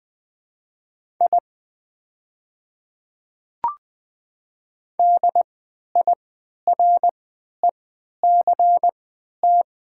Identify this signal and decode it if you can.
Morse code, heard as a keyed tone, about 20 wpm, spaced as for 11 wpm, 700 Hz. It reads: I DIRECT
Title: 3x Character Spacing - Top 500 Words - 20wpm